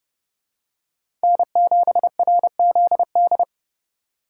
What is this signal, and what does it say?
Morse code, heard as a keyed tone, about 30 wpm, 700 Hz. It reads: N7RZD
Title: USA Call Signs - 4x Character Spacing - 30wpm